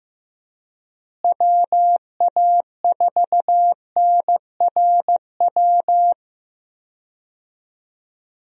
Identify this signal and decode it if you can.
Morse code, heard as a keyed tone, about 15 wpm, 700 Hz. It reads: WA4NRW
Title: USA Call Signs - 6 Characters - Repeated 3x - 15wpm